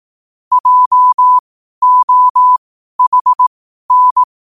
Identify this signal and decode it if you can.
Morse code, heard as a keyed tone, about 18 wpm, spaced as for 12 wpm, 1000 Hz. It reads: JOHN
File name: qotdfiles / 12wpm